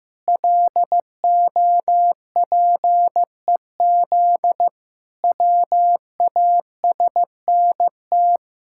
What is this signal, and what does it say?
Morse code, heard as a keyed tone, about 15 wpm, 700 Hz. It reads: LOPEZ WASNT
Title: Pangram Challenge v2 - 15wpm